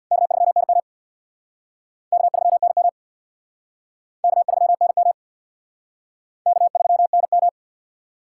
Morse code, heard as a keyed tone, about 59 wpm, 700 Hz. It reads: K3NM K3NM K3NM K3NM
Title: Speed Racing - USA 1x2 Call Signs 50wpm